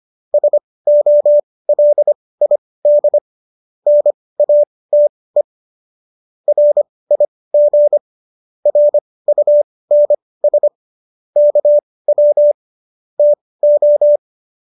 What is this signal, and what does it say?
Morse code, heard as a keyed tone, about 25 wpm, spaced as for 18 wpm, 590 Hz. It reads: SOLID NATE RIG RUNS KW TO
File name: qso211 18